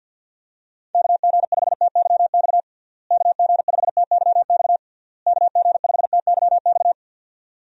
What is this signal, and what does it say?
Morse code, heard as a keyed tone, about 52 wpm, 700 Hz. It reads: KG5TYX KG5TYX KG5TYX
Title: USA Call Signs - 6 Characters - Repeated 3x - 50wpm